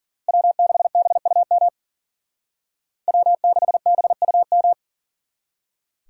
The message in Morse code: W6BUM W6BUM

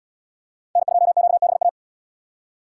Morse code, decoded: N2QBA

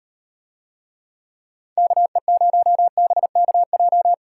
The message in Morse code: KE0BKJ

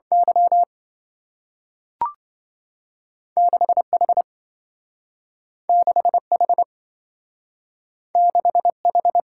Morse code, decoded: Y 6H 6H 6H

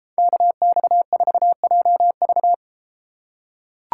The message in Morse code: KX4JV

